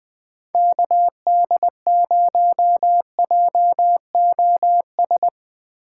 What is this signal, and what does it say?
Morse code, heard as a keyed tone, about 20 wpm, 700 Hz. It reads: KD0JOS
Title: USA Call Signs - 6 Characters - 2x Character Spacing - 20wpm